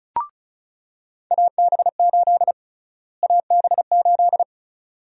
Morse code, heard as a keyed tone, about 36 wpm, 700 Hz. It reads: AB8 AB8
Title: USA Call Sign Prefixes - Repeated 2x - 35wpm